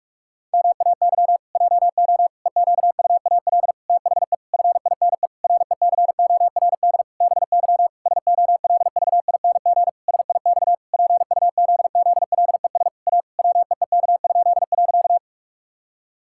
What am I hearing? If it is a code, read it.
Morse code, heard as a keyed tone, about 41 wpm, 700 Hz. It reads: MAY JO EQUAL THE FINE RECORD BY SOLVING SIX PUZZLES A WEEK?.
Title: Pangram Challenge v2 - 45wpm